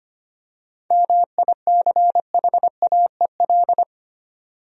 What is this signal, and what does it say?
Morse code, heard as a keyed tone, about 25 wpm, 700 Hz. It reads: MICHAEL
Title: USA Common Names - 25wpm